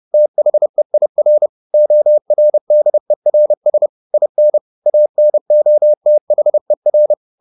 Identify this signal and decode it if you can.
Morse code, heard as a keyed tone, about 30 wpm, 600 Hz. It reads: THEIR ORDERS IN ANOTHER
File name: the old man and the sea / the old man and the sea 1 30wpm